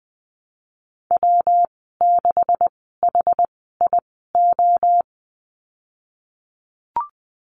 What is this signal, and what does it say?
Morse code, heard as a keyed tone, about 20 wpm, spaced as for 15 wpm, 700 Hz. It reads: W6HIO
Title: 147 Lesson 24g - Letter y - Call Signs 20wpm